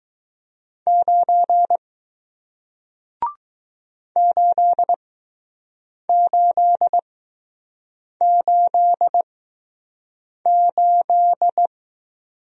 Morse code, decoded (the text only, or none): 9 8888